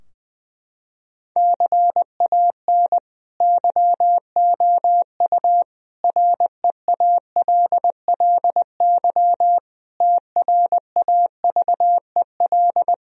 Morse code, heard as a keyed tone, about 20 wpm, 700 Hz. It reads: CAN YOU REALLY TRAVEL
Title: Sentences from Top 1000 Words - Review Part 2 - 20wpm